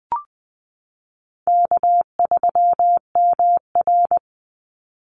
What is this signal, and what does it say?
Morse code, heard as a keyed tone, about 20 wpm, 700 Hz. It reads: K3MR
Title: Call Signs Repeated 3x - 20wpm